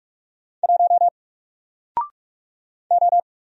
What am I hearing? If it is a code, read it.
Morse code, heard as a keyed tone, about 44 wpm, 700 Hz. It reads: 1 O
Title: Single Character (Letter:Number:Punctuation:Prosign) 45wpm